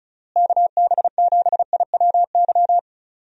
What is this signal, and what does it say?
Morse code, heard as a keyed tone, about 34 wpm, 700 Hz. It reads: KB7IWY
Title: USA Call Signs - 6 Characters - Repeated 3x - 35wpm